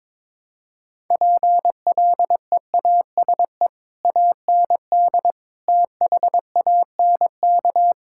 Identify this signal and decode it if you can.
Morse code, heard as a keyed tone, about 22 wpm, 700 Hz. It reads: PLEASE AND THANK
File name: Difficult Sentences with Hints - 22wpm